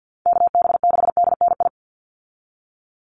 Morse code, heard as a keyed tone, about 50 wpm, 700 Hz. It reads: KB6DNI
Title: USA Call Signs 50wpm